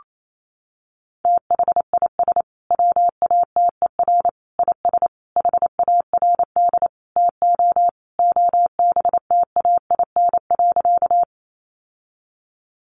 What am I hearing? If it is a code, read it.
Morse code, heard as a keyed tone, about 28 wpm, 700 Hz. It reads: THIS WATER IS HARD TO OBTAIN.
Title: Sentences from Top 1300 Words 28wpm